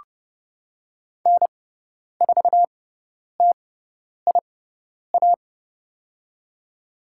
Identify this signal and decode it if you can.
Morse code, heard as a keyed tone, about 30 wpm, spaced as for 10 wpm, 700 Hz. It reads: N4TIA
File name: Lesson 18g - Letter m - Call Signs 30wpm